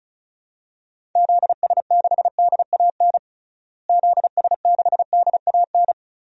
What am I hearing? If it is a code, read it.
Morse code, heard as a keyed tone, about 34 wpm, 700 Hz. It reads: ZS6DAN ZS6DAN
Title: DXCC-100 Call Signs - 6 Characters - Repeated 3x - 35wpm